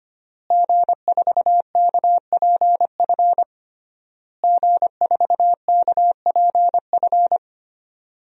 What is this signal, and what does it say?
Morse code, heard as a keyed tone, about 25 wpm, 700 Hz. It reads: G4KPF G4KPF
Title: DXCC-100 Call Signs - 5 Characters - Repeated 3x - 25wpm